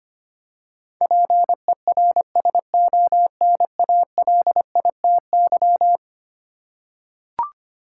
Morse code, heard as a keyed tone, about 25 wpm, 700 Hz. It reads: PERSONALITY
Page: morse-code-ninja-practice-25wpm